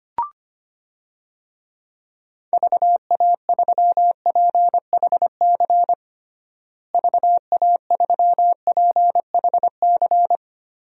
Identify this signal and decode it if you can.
Morse code, heard as a keyed tone, about 25 wpm, 700 Hz. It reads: VA3PHC VA3PHC
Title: DXCC-100 Call Signs - 6 Characters - Repeated 3x - 25wpm